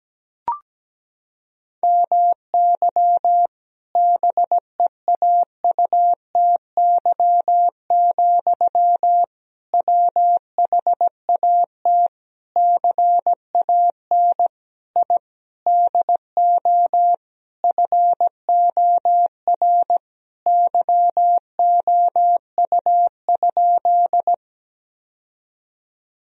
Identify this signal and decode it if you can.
Morse code, heard as a keyed tone, about 17 wpm, 700 Hz. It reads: MY BEAUTY, WHAT CAN I DO FOR YOU?
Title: Sentences from Top 1000 Words - Review Part 2 - 17wpm